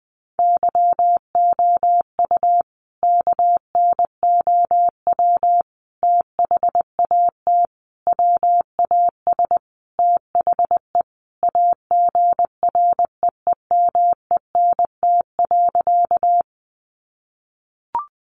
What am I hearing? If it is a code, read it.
Morse code, heard as a keyed tone, about 20 wpm, 700 Hz. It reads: YOU KNOW THAT WAS THE AGREEMENT.
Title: Sentences from Top 1500 Words 20wpm